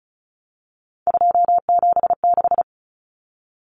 Morse code, heard as a keyed tone, about 34 wpm, 700 Hz. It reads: 276
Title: Common Cut Numbers - 3-Digit 35wpm